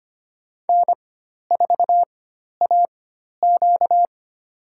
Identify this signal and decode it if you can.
Morse code, heard as a keyed tone, about 25 wpm, spaced as for 12 wpm, 700 Hz. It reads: N4AQ
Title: USA Call Signs - 4 Characters - 4x Character Spacing - 25wpm